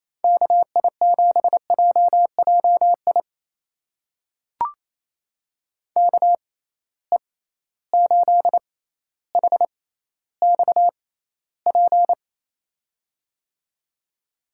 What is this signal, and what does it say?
Morse code, heard as a keyed tone, about 27 wpm, spaced as for 14 wpm, 700 Hz. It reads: KI7JJI KE8HXP
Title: USA Call Signs - 6 Characters - 6x Character Spacing - 28wpm